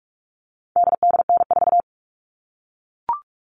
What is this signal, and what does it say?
Morse code, heard as a keyed tone, about 44 wpm, 700 Hz. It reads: DDN4
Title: All Cut Numbers - 4-Digit 45wpm